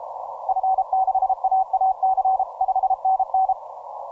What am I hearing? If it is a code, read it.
Morse code, heard as a keyed tone, about 31 wpm, 760 Hz. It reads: F6AAC 5NN
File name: simple-file-list / 9X0T CW 10M